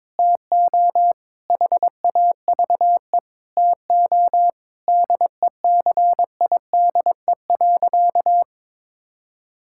Morse code, heard as a keyed tone, about 22 wpm, 700 Hz. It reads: TO HAVE TO DECIDE.
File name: practice / 22wpm / Sentences with 4-Word Phrases - 22wpm